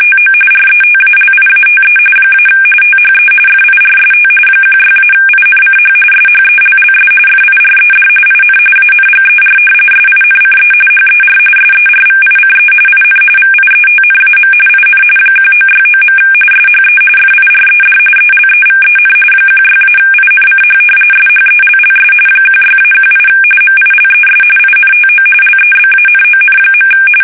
RTTY, about 45.45 baud. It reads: TO RETALIATE - THE FIRST 16-INCH SHELLS FIRED IN ANGER SINCE HER SERVICE IN VIETNAM IN 1969. SHE FIRED ELEVEN 16-INCH SHELLS 16-18 MILES INTO LEBANON.
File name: fsk850 / battleshipnj